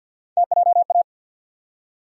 Morse code, heard as a keyed tone, about 52 wpm, 700 Hz. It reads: TJA
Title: USA Call Sign Postfixes - 3 Letters 50wpm